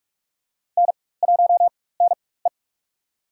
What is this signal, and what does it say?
Morse code, heard as a keyed tone, about 44 wpm, spaced as for 21 wpm, 690 Hz. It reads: N1NE
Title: USA Call Signs - 4x Character Spacing - 45wpm